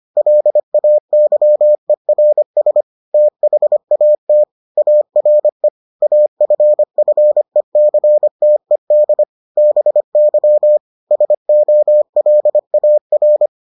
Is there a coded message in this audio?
Yes: Morse code from 0.2 to 13.5 s, heard as a keyed tone, about 25 wpm, 600 Hz. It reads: LAYERS THAT ARE AFFECTED BY SOLAR